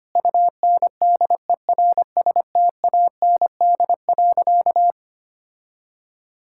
Morse code, heard as a keyed tone, about 25 wpm, 700 Hz. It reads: UNDERSTAND.
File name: practice / 25wpm / Sentences with 4-Word Phrases - 25wpm